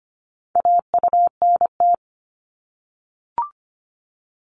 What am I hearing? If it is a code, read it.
Morse code, heard as a keyed tone, about 25 wpm, 700 Hz. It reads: AUNT
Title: Multi-letter Phonemes 25wpm